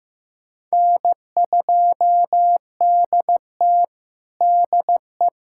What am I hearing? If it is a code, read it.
Morse code, heard as a keyed tone, about 16 wpm, 700 Hz. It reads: N2DT DE
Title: USA Callsign DE Callsign - 4 Characters - 15wpm